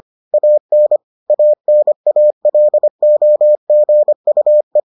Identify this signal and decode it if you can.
Morse code, heard as a keyed tone, about 25 wpm, 600 Hz. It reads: AN ANALOGUE